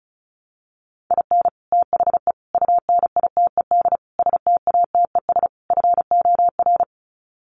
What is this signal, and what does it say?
Morse code, heard as a keyed tone, about 34 wpm, 700 Hz. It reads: IN THE UNITED STATES FOR
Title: Sets of 5 Words Encore2 35wpm